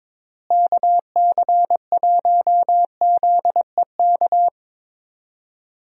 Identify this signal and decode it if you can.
Morse code, heard as a keyed tone, about 22 wpm, 700 Hz. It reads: KC1ZEK
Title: USA Call Signs - 6 Characters - Repeated 3x - 22wpm